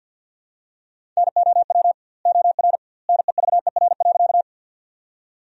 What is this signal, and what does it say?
Morse code, heard as a keyed tone, about 51 wpm, 700 Hz. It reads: NOW OR NEVER.
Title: 2x Word-Spacing - Sentences from Top 300 Words - 50wpm